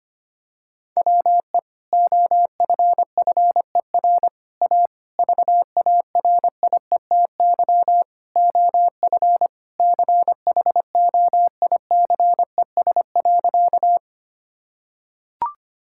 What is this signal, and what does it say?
Morse code, heard as a keyed tone, about 25 wpm, 700 Hz. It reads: WE OFFER A VARIETY OF CHOICES.
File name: Sentences with 3-Word Phrases - 25wpm